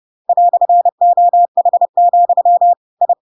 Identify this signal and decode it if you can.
Morse code, heard as a keyed tone, about 30 wpm, 700 Hz. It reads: "OH, I
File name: Pinocchio / Pinocchio ch1 30 30 ik1hge